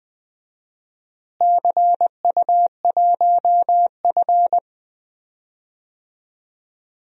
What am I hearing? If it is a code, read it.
Morse code, heard as a keyed tone, about 20 wpm, 700 Hz. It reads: CU1F
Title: DXCC-100 Call Signs - 4 Characters - Repeated 2x - 20wpm